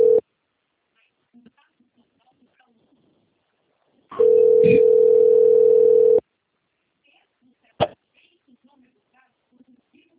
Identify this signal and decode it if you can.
Ringback tone (North America), 440 + 480 Hz, with speech.
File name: out- - - - - .13945